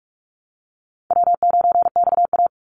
Morse code, heard as a keyed tone, about 46 wpm, 700 Hz. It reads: W9XA
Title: Easy Call Signs - 45wpm